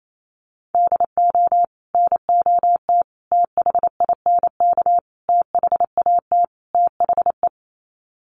Morse code, heard as a keyed tone, about 28 wpm, 700 Hz. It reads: DO NOT THINK THAT THE
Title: Sets of 5 Words Encore2 28wpm